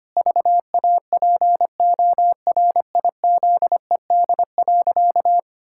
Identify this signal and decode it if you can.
Morse code, heard as a keyed tone, about 25 wpm, 700 Hz. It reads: VAPORIZED.
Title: Pangram Challenge v2 - 25wpm